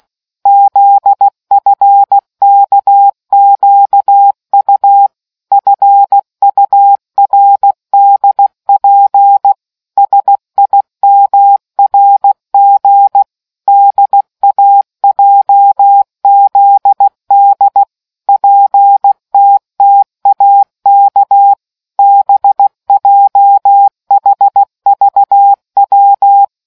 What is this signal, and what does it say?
Morse code, heard as a keyed tone, about 16 wpm, 800 Hz. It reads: ZFKQU FURDP SIMRG DAJZD PTTAK BJHVW